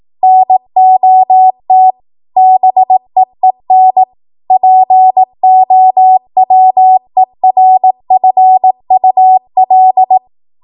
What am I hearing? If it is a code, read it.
Morse code, heard as a keyed tone, about 18 wpm, 750 Hz. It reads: NOT BEEN POWERFUL
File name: Morse / 18WPM